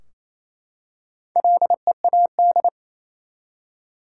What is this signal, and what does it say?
Morse code, heard as a keyed tone, about 28 wpm, 700 Hz. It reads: LEAD
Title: Four Letter Words - 28wpm